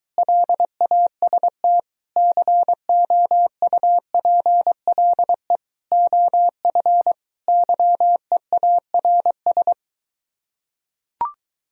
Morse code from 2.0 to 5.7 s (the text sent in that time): COUPLE